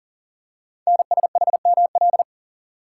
Morse code, heard as a keyed tone, about 40 wpm, 700 Hz. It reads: NSHML
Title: Code Groups - 5 Letters - 40wpm